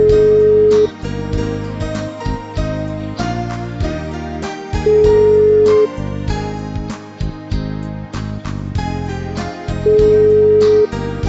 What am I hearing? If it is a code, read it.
Ringback tone (Europe, 425 Hz), with noise.